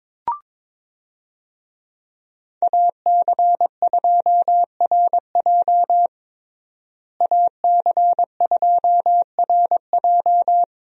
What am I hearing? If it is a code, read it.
Morse code, heard as a keyed tone, about 22 wpm, 700 Hz. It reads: AC2RJ AC2RJ
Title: USA Call Signs - 5 Characters - Repeated 3x - 22wpm